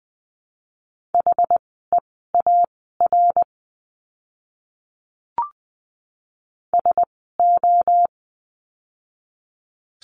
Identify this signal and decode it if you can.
Morse code, heard as a keyed tone, about 20 wpm, spaced as for 14 wpm, 700 Hz. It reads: HEAR SO
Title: 108 Lesson 18f - Letter m - Words 20wpm